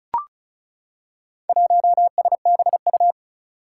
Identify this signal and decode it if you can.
Morse code, heard as a keyed tone, about 36 wpm, 700 Hz. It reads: 1SBU
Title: Four Challenging Letter-Number Combinations 35wpm